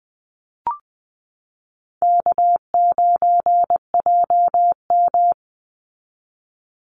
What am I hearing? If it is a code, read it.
Morse code, heard as a keyed tone, about 20 wpm, 700 Hz. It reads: K9JM
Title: Easy Call Signs Repeated 3x - 20wpm